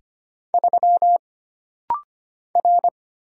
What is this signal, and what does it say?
Morse code, heard as a keyed tone, about 25 wpm, 700 Hz. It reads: 3 R
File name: Single Character (Letter:Number:Punctuation:Prosign) - Rapid-Fire - 25wpm